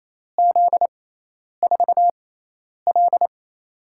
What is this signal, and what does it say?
Morse code, heard as a keyed tone, about 28 wpm, spaced as for 10 wpm, 700 Hz. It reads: Z4L
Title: USA Call Signs - 4 Characters - 6x Character Spacing - 28wpm